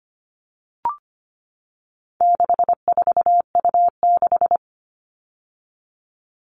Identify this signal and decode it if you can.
Morse code, heard as a keyed tone, about 25 wpm, 700 Hz. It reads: 64U6
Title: All Cut Numbers - 4-Digit 25wpm